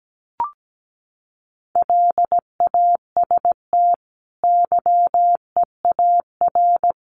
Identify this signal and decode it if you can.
Morse code, heard as a keyed tone, about 17 wpm, 700 Hz. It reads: LAST YEAR
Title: Sets of 2 Words 17wpm